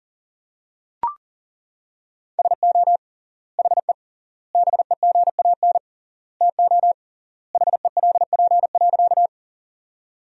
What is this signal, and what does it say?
Morse code, heard as a keyed tone, about 40 wpm, 700 Hz. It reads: SO HE BEGAN TO HELP.
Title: 3x Word-Spacing - Sentences from Top 300 Words - 40wpm